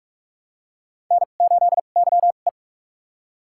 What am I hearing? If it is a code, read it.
Morse code, heard as a keyed tone, about 46 wpm, spaced as for 33 wpm, 700 Hz. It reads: N8YE
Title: USA Call Signs - 4 Characters - 2x Character Spacing - 45wpm